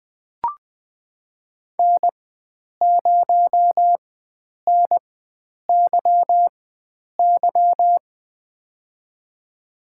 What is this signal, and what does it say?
Morse code, heard as a keyed tone, about 21 wpm, spaced as for 9 wpm, 700 Hz. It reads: N0NYY
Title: USA Call Signs - 5 Characters - 4x Character Spacing - 20wpm